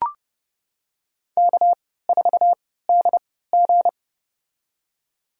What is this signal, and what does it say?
Morse code, heard as a keyed tone, about 30 wpm, spaced as for 17 wpm, 700 Hz. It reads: K4DG
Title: USA Call Signs - 3x Character Spacing - 30wpm